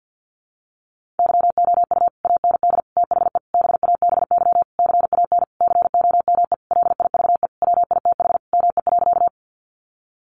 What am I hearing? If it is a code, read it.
Morse code, heard as a keyed tone, about 48 wpm, 700 Hz. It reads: YOU AND THE BABY CAN COME LIVE WITH ME.
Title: Sentences from Top 1300 Words 50wpm